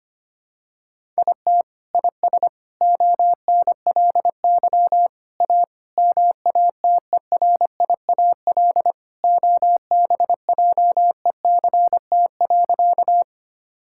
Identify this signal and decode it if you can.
Morse code, heard as a keyed tone, about 25 wpm, 700 Hz. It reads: IT IS ONLY A MATERIAL OBJECT.
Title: Sentences from Top 1000 Words - Review Part 5 - 25wpm